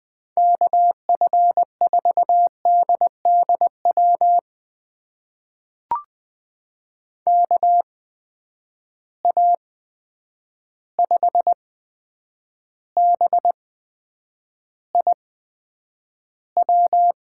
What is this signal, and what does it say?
Morse code, heard as a keyed tone, about 20 wpm, spaced as for 9 wpm, 700 Hz. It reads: KF4DDW KA5BIW
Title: USA Call Signs - 6 Characters - 8x Character Spacing - 20wpm